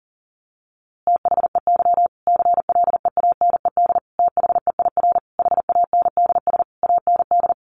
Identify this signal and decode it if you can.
Morse code, heard as a keyed tone, about 40 wpm, 700 Hz. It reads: THEY CLEANED THEIR HANDS AND
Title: A Sound Of Thunder 40wpm